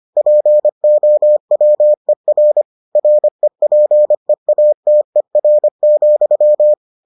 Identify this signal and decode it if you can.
Morse code, heard as a keyed tone, about 25 wpm, 600 Hz. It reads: POWER REPEATER,